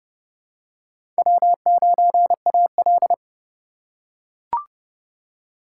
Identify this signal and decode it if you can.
Morse code, heard as a keyed tone, about 30 wpm, 700 Hz. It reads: W9AL
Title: USA Call Signs - 4x Character Spacing - 30wpm